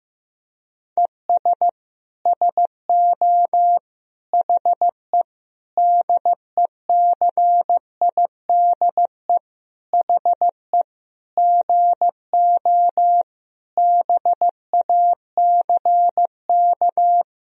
Morse code, heard as a keyed tone, about 15 wpm, 700 Hz. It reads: ES SO HE DECIDE HE GO BACK
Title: Sentences from Top 500 Words and Abbreviations 15wpm